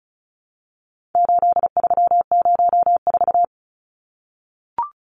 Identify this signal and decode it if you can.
Morse code, heard as a keyed tone, about 34 wpm, 700 Hz. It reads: 8304
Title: Numbers - 4-Digit 35wpm